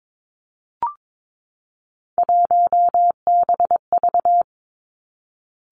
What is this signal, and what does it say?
Morse code, heard as a keyed tone, about 22 wpm, 700 Hz. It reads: 1BV
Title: Three Challenging Letter-Number Combinations 22wpm